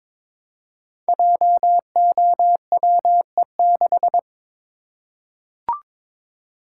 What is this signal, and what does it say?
Morse code, heard as a keyed tone, about 22 wpm, 700 Hz. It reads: JOWE6
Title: Code Groups - 5 Letter:Numbers - 22wpm